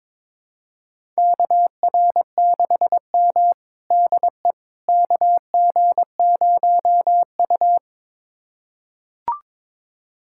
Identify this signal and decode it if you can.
Morse code, heard as a keyed tone, about 21 wpm, 700 Hz. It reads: KR6M DE KG0U
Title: USA Callsign DE Callsign - 4 Characters - 22wpm